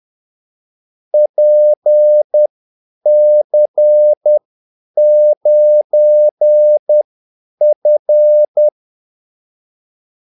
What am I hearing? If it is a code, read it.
Morse code, heard as a keyed tone, about 10 wpm, spaced as for 8 wpm, 600 Hz. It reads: PC9F
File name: Callsigns-10-8wpm